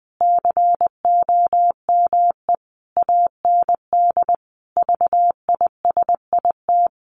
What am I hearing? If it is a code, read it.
Morse code, heard as a keyed tone, about 19 wpm, 700 Hz. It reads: COME AND VISIT
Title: Sentences from Top 1400 Words 20wpm